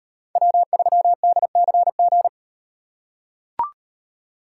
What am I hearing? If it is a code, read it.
Morse code, heard as a keyed tone, about 38 wpm, 700 Hz. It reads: W3DCG